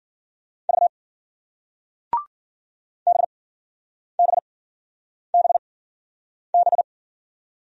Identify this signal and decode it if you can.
Morse code, heard as a keyed tone, about 56 wpm, 700 Hz. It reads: V B B B B